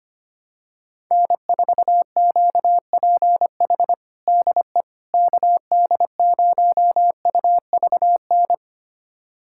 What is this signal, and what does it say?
Morse code, heard as a keyed tone, about 25 wpm, 700 Hz. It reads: N4QPH DE KD0UVN
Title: USA Callsign DE Callsign 25wpm